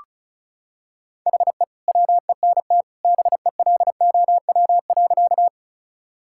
Morse code, heard as a keyed tone, about 36 wpm, 700 Hz. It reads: HE WENT BELOW.
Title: Sentences from Top 1000 Words - Review Part 5 - 35wpm